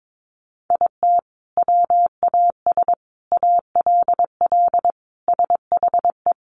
Morse code, heard as a keyed tone, about 22 wpm, 700 Hz. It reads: IT WAS ALL SHE
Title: Sentences from Top 400 Words 22wpm